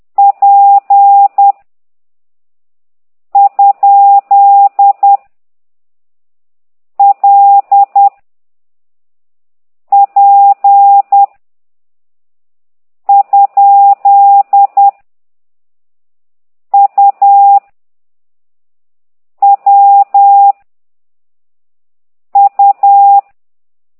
Morse code, heard as a keyed tone, about 10 wpm, 800 Hz. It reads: P ? L P ? U W U
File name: ct2idw / exerc8